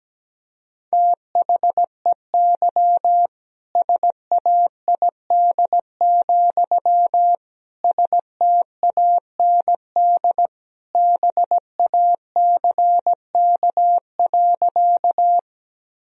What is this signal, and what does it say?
Morse code, heard as a keyed tone, about 17 wpm, 700 Hz. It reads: THEY SAID, STAND BACK.
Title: Sentences from Top 1000 Words - Review Part 1 - 17wpm